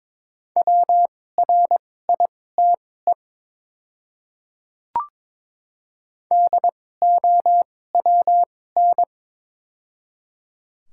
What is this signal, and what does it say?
Morse code, heard as a keyed tone, about 22 wpm, spaced as for 16 wpm, 700 Hz. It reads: WRITE DOWN
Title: 2x Character Spacing - Top 100 Words - 22wpm